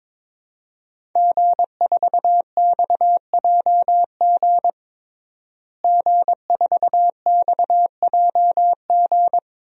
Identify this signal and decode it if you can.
Morse code, heard as a keyed tone, about 22 wpm, 700 Hz. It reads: G4XJG G4XJG
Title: DXCC-100 Call Signs - 5 Characters - Repeated 3x - 22wpm